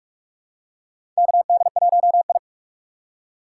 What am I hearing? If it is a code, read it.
Morse code, heard as a keyed tone, about 46 wpm, 700 Hz. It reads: KD1I
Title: USA Call Signs - 4 Characters - 4x Character Spacing - 45wpm